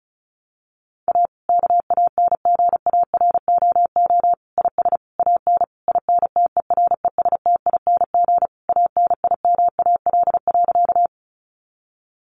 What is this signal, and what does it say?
Morse code, heard as a keyed tone, about 34 wpm, 700 Hz. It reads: A KANGAROO IS AN INTERESTING ANIMAL.
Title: Easy Phrases 35wpm